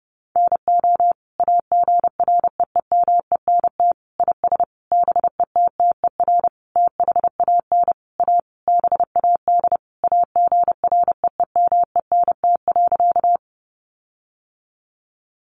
Morse code, heard as a keyed tone, about 30 wpm, 700 Hz. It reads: NO AGREEMENT IS BETTER THAN A BAD AGREEMENT.
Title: Sentences from Top 1500 Words 30wpm